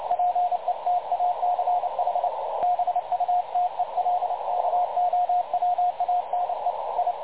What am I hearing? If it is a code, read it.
Morse code, heard as a keyed tone, about 29 wpm, 720 Hz, with radio noise. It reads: PETACH BUTELKOWANE I
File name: dlazaawansowanych / nagraniazpasma / websdr recording 3565.0kHz